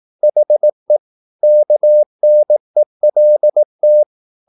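Morse code, heard as a keyed tone, about 18 wpm, 600 Hz. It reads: HE KNELT
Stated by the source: Morse practice at 18 wpm, English text, from CW med Gnister.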